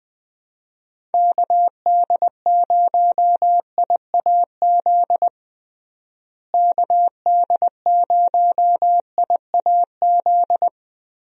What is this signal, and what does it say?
Morse code, heard as a keyed tone, about 20 wpm, 700 Hz. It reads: KD0IAZ KD0IAZ
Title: USA Call Signs - 6 Characters - Repeated 2x - 20wpm